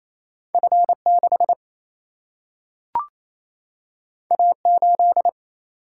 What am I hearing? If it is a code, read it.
Morse code, heard as a keyed tone, about 28 wpm, 700 Hz. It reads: F6 A8
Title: Single-Letter Single-Number 28wpm